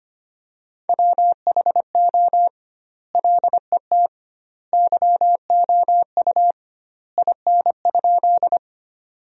Morse code, heard as a keyed tone, about 25 wpm, 700 Hz. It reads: WHO LET YOU IN?
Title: 2x Word-Spacing - Sentences from Top 300 Words - 25wpm